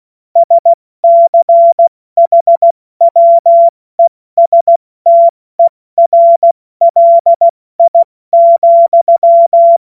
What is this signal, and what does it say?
Morse code, heard as a keyed tone, about 16 wpm, 680 Hz. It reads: SCHWESTERLI,
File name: Morsekurs / Witze / Witz 07. September 2021, 70 F80